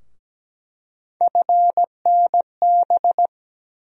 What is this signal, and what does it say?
Morse code, heard as a keyed tone, about 17 wpm, 700 Hz. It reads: FNB
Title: USA Call Sign Postfixes - 3 Letters 17wpm